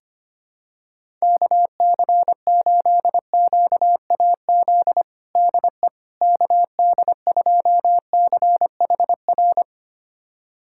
Morse code, heard as a keyed tone, about 25 wpm, 700 Hz. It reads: KC8QAZ DE KD2CHR
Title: USA Callsign DE Callsign - 6 Characters - 25wpm